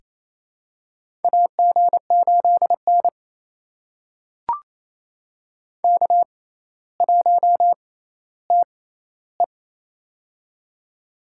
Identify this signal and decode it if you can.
Morse code, heard as a keyed tone, about 29 wpm, spaced as for 15 wpm, 700 Hz. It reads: AG8N K1TE
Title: USA Call Signs - 4 Characters - 6x Character Spacing - 28wpm